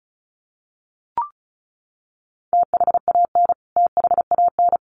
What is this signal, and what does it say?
Morse code, heard as a keyed tone, about 34 wpm, 700 Hz. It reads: THAN THAN
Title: Top 200 Words Repeated 35wpm